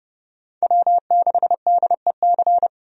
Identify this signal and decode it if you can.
Morse code, heard as a keyed tone, about 30 wpm, 700 Hz. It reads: W6DEC
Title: USA Call Signs - 5 Characters - 30wpm